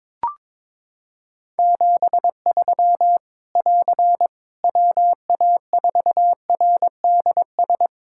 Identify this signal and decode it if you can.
Morse code, heard as a keyed tone, about 22 wpm, 700 Hz. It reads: 73 <AR> WA4RDS
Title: K1USN SST QSO Practice 22wpm